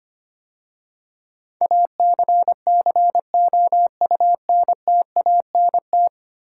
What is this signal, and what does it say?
Morse code, heard as a keyed tone, about 25 wpm, 700 Hz. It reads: ACCOUNTANT
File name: 10 Letter Words - First 3 Letters Given - 25wpm